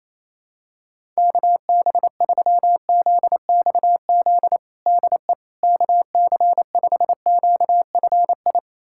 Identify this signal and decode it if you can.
Morse code, heard as a keyed tone, about 28 wpm, 700 Hz. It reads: KB3ZXZ DE KC5QFI
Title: USA Callsign DE Callsign - 6 Characters - 28wpm